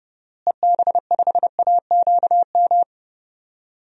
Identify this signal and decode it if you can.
Morse code, heard as a keyed tone, about 28 wpm, 700 Hz. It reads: EB5AQM
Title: DXCC-100 Call Signs - 6 Characters - Repeated 3x - 30wpm